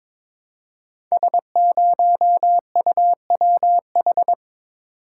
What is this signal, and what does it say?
Morse code, heard as a keyed tone, about 22 wpm, 700 Hz. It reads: S0UWH
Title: Code Groups - 5 Letter:Numbers - 22wpm